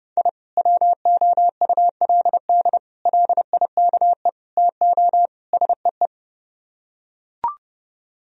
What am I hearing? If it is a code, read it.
Morse code, heard as a keyed tone, about 31 wpm, 700 Hz. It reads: I WOULD LIKE TO SEE